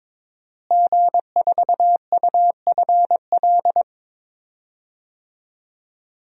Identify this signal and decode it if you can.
Morse code, heard as a keyed tone, about 22 wpm, 700 Hz. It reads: G4UFL
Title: DXCC-100 Call Signs - 5 Characters - Repeated 3x - 22wpm